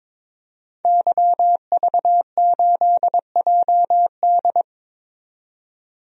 Morse code, heard as a keyed tone, about 21 wpm, 700 Hz. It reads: YV8JD
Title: DXCC-100 Call Signs - 5 Characters - 22wpm